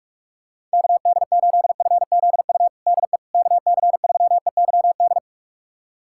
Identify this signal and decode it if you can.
Morse code, heard as a keyed tone, about 42 wpm, 700 Hz. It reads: KD8FZU DE KC3EYD
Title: USA Callsign DE Callsign 45wpm